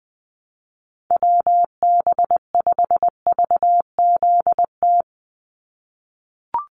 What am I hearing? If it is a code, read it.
Morse code, heard as a keyed tone, about 20 wpm, 700 Hz. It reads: WB5VZT
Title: USA Call Signs 20wpm